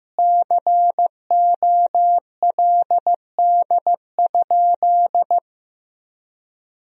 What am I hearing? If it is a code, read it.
Morse code, heard as a keyed tone, about 16 wpm, 700 Hz. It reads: COLD?
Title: Sentences from Top 400 Words and Abbreviations 15wpm